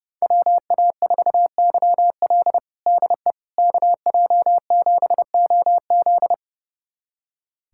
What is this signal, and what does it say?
Morse code, heard as a keyed tone, about 32 wpm, 700 Hz. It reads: WA4YL DE KJ7OZ
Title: USA Callsign DE Callsign 30wpm